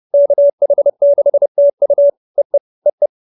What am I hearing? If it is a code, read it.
Morse code, heard as a keyed tone, about 30 wpm, 570 Hz. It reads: KH6TU EE EE
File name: QSO309 30